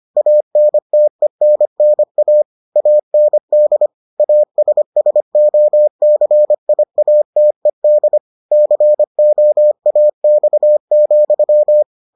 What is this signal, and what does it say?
Morse code, heard as a keyed tone, about 25 wpm, 600 Hz. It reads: ANTENNA AND ASSOCIATED COAX,